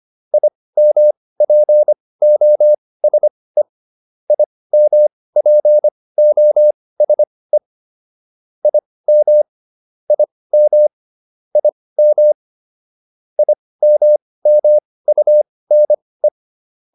Morse code, heard as a keyed tone, about 25 wpm, spaced as for 18 wpm, 600 Hz. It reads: IMPOSE IMPOSE IM IM IM IMMUNE